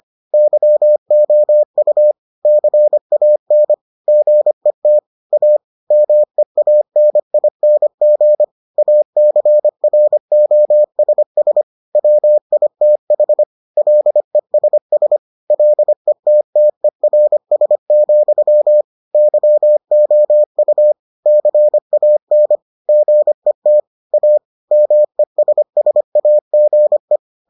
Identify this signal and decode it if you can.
Morse code, heard as a keyed tone, about 25 wpm, 600 Hz. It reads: YOU CAN GET A MEANING ACROSS WITH LESS LETTERS, YOU CAN GET A MESSAGE